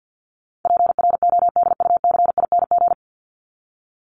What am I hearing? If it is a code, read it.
Morse code, heard as a keyed tone, about 50 wpm, 700 Hz. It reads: PRODUCING
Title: ING Suffix - Top 100 Words with Base Words 50wpm